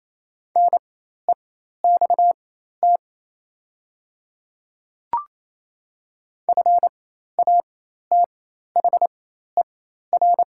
Morse code, heard as a keyed tone, about 28 wpm, spaced as for 13 wpm, 700 Hz. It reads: NEXT FATHER
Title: 4x Character Spacing - Top 300 Words - 28wpm